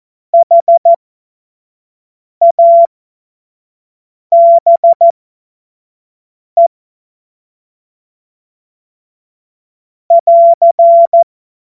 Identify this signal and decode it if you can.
Morse code, heard as a keyed tone, about 14 wpm, spaced as for 5 wpm, 680 Hz. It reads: HABE <AR>
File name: Morsekurs / Witze / Witz 17. Mai 2022 25 F70